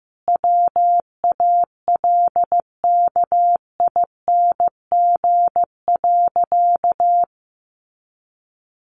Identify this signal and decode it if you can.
Morse code, heard as a keyed tone, about 14 wpm, 700 Hz. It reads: WALKING.
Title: Easy Phrases 15wpm